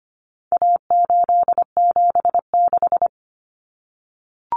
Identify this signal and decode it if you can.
Morse code, heard as a keyed tone, about 25 wpm, 700 Hz. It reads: A876
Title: Common Cut Numbers - 4-Digit 25wpm